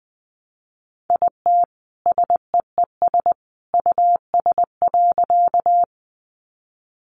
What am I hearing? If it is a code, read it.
Morse code, heard as a keyed tone, about 20 wpm, 700 Hz. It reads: IT SEES US.
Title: A Sound Of Thunder 20wpm